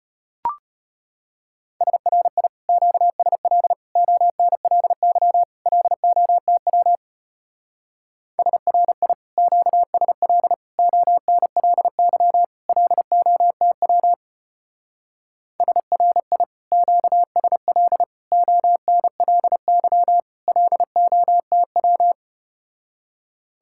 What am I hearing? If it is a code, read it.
Morse code, heard as a keyed tone, about 33 wpm, 700 Hz. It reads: SRI QSL ONLY LOTW SRI QSL ONLY LOTW SRI QSL ONLY LOTW